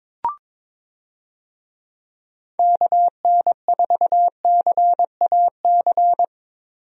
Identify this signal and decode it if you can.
Morse code, heard as a keyed tone, about 22 wpm, 700 Hz. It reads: KN4CAC
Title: USA Call Signs - 6 Characters - Repeated 2x - 22wpm